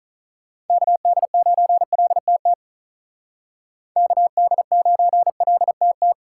Morse code, heard as a keyed tone, about 43 wpm, 700 Hz. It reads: KD9LTT KD9LTT
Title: Speed Racing - USA 2x3 Call Signs 35wpm